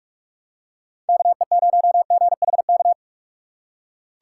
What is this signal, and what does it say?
Morse code, heard as a keyed tone, about 42 wpm, 700 Hz. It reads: KE0GHK